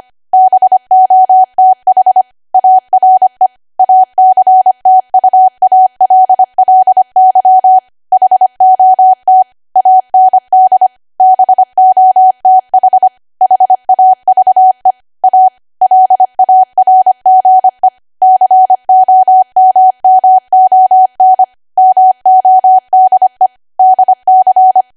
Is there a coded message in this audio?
Yes: Morse code from 0.3 to 24.8 s, heard as a keyed tone, about 25 wpm, 750 Hz. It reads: BOTH ARE ACTUALLY HOT AND BOTH HAVE A LARGE COMMON MODE DC